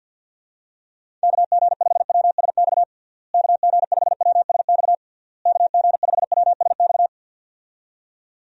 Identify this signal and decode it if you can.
Morse code, heard as a keyed tone, about 50 wpm, 700 Hz. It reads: KG5WSX KG5WSX KG5WSX
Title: USA Call Signs - 6 Characters - Repeated 3x - 50wpm